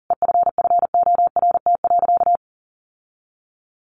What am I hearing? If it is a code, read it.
Morse code, heard as a keyed tone, about 40 wpm, 700 Hz. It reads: EFFORT.
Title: Sentences from Top 1400 Words 40wpm